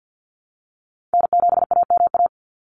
Morse code, heard as a keyed tone, about 49 wpm, 700 Hz. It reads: N7AMA
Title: NA SOTA Chaser Callsigns 50wpm